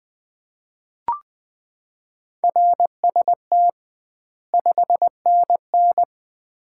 Morse code, heard as a keyed tone, about 20 wpm, 700 Hz. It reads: RST 5NN
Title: Common QSO Phrases - 2x Word-Spacing - 20wpm